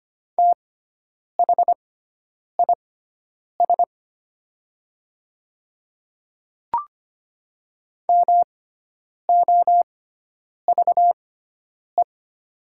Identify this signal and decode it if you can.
Morse code, heard as a keyed tone, about 25 wpm, spaced as for 9 wpm, 700 Hz. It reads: THIS MOVE
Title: 6x Character Spacing - Top 300 Words - 25wpm